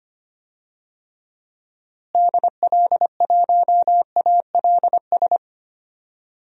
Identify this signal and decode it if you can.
Morse code, heard as a keyed tone, about 25 wpm, 700 Hz. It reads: DL1ALS
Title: DXCC-100 Call Signs - 6 Characters - Repeated 3x - 25wpm